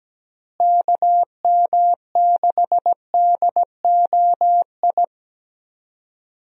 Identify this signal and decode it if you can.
Morse code, heard as a keyed tone, about 17 wpm, 700 Hz. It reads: KM6DOI
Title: USA Call Signs - 6 Characters - Repeated 3x - 17wpm